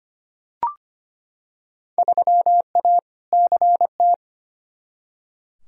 Morse code, heard as a keyed tone, about 25 wpm, 700 Hz. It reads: 3A CT
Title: Field Day Exchanges - 25wpm